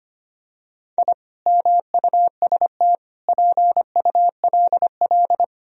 Morse code, heard as a keyed tone, about 25 wpm, 700 Hz. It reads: I MUST PULL
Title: Sentences from Top 1000 Words - Review Part 2 - 25wpm